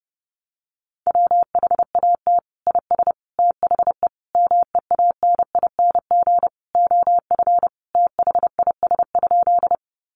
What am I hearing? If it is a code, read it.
Morse code, heard as a keyed tone, about 28 wpm, 700 Hz. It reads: WHAT IS THE MEANING OF THIS?
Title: Sentences from Top 900 Words 30wpm